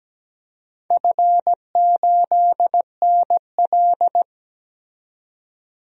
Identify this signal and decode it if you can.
Morse code, heard as a keyed tone, about 17 wpm, 700 Hz. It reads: F8NL
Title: DXCC-100 Call Signs - 4 Characters - 17wpm